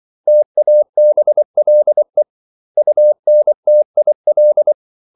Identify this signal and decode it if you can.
Morse code, heard as a keyed tone, about 24 wpm, 600 Hz. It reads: TABLE UNTIL
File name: CW / Practice / 500MostCommonEnglishWords24